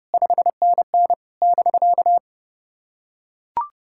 Morse code, heard as a keyed tone, about 30 wpm, 690 Hz. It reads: 5NN <BK>
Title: SOTA QSO Practice 30wpm